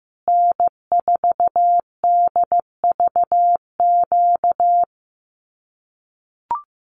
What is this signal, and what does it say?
Morse code, heard as a keyed tone, about 15 wpm, 700 Hz. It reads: N4DVQ
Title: Common QSO Phrases Encore 15wpm